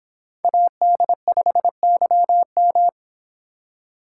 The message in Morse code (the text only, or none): AD5YM